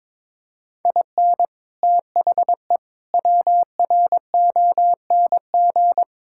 Morse code, heard as a keyed tone, about 22 wpm, 700 Hz. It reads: IN THE WRONG
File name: practice / 22wpm / Sets of 8 Words - 22wpm